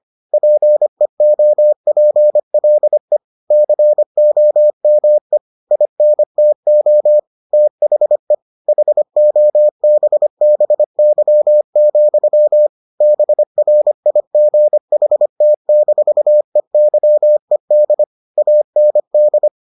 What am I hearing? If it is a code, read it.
Morse code, heard as a keyed tone, about 25 wpm, 600 Hz. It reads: PEOPLE COME INTO THE HOBBY, BRIGHT-EYED AND